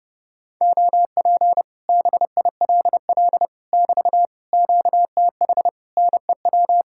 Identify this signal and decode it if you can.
Morse code, heard as a keyed tone, about 29 wpm, 700 Hz. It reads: OP BILL <BT> QTH NEW